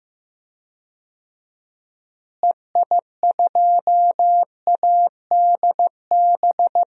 Morse code, heard as a keyed tone, about 15 wpm, 700 Hz. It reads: EI2ADB
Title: DXCC-100 Call Signs - 6 Characters - Repeated 3x - 15wpm